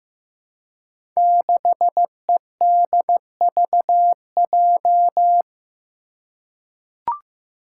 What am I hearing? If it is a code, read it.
Morse code, heard as a keyed tone, about 15 wpm, 700 Hz. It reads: 6EDVJ
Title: Code Groups - 5 Letter:Numbers - 15wpm